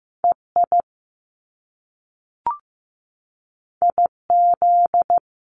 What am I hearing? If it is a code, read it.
Morse code, heard as a keyed tone, about 15 wpm, 700 Hz. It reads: EI IZ
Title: Top Two Letter Combinations in Words 15wpm